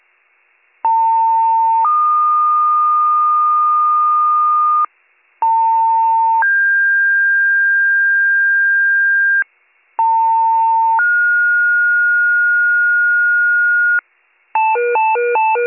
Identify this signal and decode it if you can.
Two-tone page, heard as a one-second tone, then a three-second tone, about 900 then 1220 Hz, about 900 then 1600 Hz, about 900 then 1430 Hz, with noise. It is heard from 0.8 to 14.0 s.